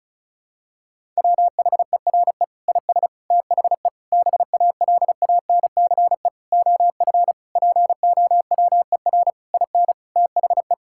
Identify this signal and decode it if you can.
Morse code, heard as a keyed tone, about 34 wpm, 700 Hz. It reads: WHERE IS THE BALANCE OF POWER IN THE
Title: Sentences from Top 1000 Words - Review Part 6 - 35wpm